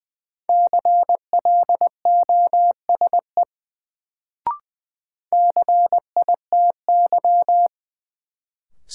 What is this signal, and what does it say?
Morse code, heard as a keyed tone, about 20 wpm, 700 Hz. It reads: CLOSE CITY
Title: Sentences from Top 300 Words - New Word Drill - 20wpm